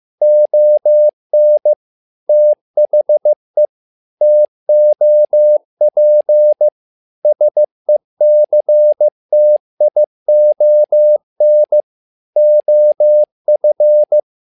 Morse code, heard as a keyed tone, about 15 wpm, 600 Hz. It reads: ON THE TOP SECTION OF
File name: learning center / tower trouble / tower trouble 15wpm